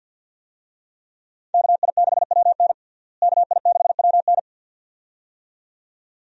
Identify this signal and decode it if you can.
Morse code, heard as a keyed tone, about 51 wpm, 700 Hz. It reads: KI6WN KI6WN
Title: USA Call Signs - 5 Characters - Repeated 2x - 50wpm